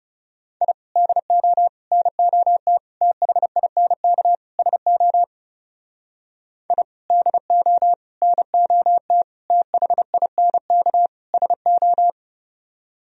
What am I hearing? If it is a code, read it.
Morse code, heard as a keyed tone, about 38 wpm, 700 Hz. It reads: I DO NOT THINK SO I DO NOT THINK SO